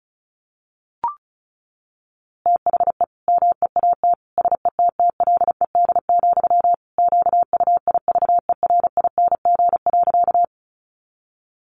Morse code, heard as a keyed tone, about 34 wpm, 700 Hz. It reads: THE MEAT SETTLED, QUIVERING.
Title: A Sound Of Thunder 35wpm